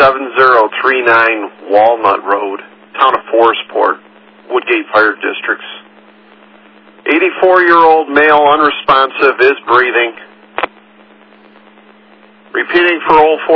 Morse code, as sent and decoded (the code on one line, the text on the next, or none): none